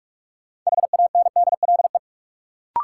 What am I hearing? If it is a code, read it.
Morse code, heard as a keyed tone, about 41 wpm, 700 Hz. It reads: HANDLE
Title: Sentences from Top 1100 Words - New Word Drill Alt - 45wpm